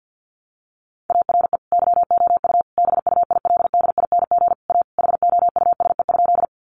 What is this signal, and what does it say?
Morse code, heard as a keyed tone, about 50 wpm, 700 Hz. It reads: ARE YOU BUILDING A HOUSE?
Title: Sentences from Top 700 Words 50wpm